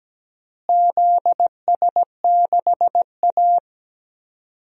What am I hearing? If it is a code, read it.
Morse code, heard as a keyed tone, about 17 wpm, 700 Hz. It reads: ZS6A
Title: DXCC-100 Call Signs - 4 Characters - Repeated 3x - 17wpm